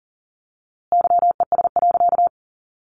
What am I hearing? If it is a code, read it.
Morse code, heard as a keyed tone, about 40 wpm, 700 Hz. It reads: YES.
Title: A Sound Of Thunder 40wpm